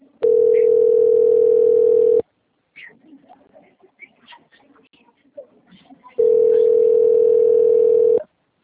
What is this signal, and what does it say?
Ringback tone (North America), 440 + 480 Hz.
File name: out- - - - - .16212